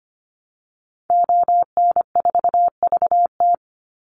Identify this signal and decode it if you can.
Morse code, heard as a keyed tone, about 25 wpm, 700 Hz. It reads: ON4VT
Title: Call Signs - 25wpm